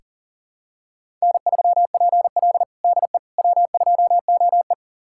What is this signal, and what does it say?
Morse code, heard as a keyed tone, about 40 wpm, 700 Hz. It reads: N3PL DE W2OE
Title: USA Callsign DE Callsign - 4 Characters - 40wpm